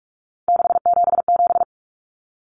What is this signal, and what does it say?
Morse code, heard as a keyed tone, about 44 wpm, 700 Hz. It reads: 677
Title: Numbers - 3-Digit 45wpm